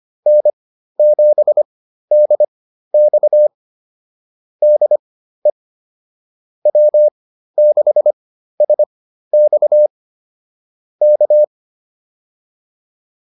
Morse code, heard as a keyed tone, about 25 wpm, spaced as for 13 wpm, 600 Hz. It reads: N7DX DE W6SX K